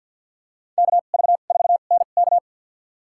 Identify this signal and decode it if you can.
Morse code, heard as a keyed tone, about 53 wpm, spaced as for 37 wpm, 700 Hz. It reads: KV4NK
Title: USA Call Signs - 2x Character Spacing - 50wpm